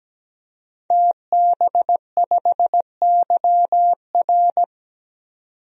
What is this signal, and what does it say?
Morse code, heard as a keyed tone, about 17 wpm, 700 Hz. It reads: TB5YR
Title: DXCC-100 Call Signs - 5 Characters - 17wpm